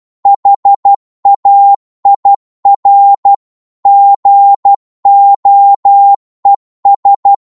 Morse code, heard as a keyed tone, about 12 wpm, 800 Hz. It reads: HAIR GOES
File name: 12wpm